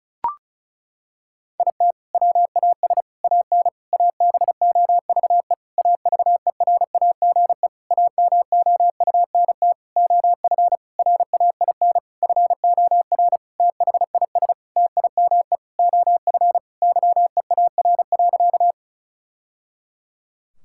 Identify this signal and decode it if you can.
Morse code, heard as a keyed tone, about 36 wpm, 700 Hz. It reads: IT WAS AN ABOVE AVERAGE AMOUNT OF RAIN FOR THIS TIME OF YEAR.
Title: Sentences from Top 1000 Words - Review Part 6 - 35wpm